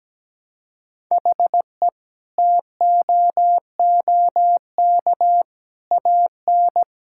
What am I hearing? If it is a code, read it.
Morse code, heard as a keyed tone, about 17 wpm, 700 Hz. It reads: HE TOOK AN
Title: Sentences from Top 1000 Words - Review Part 6 - 17wpm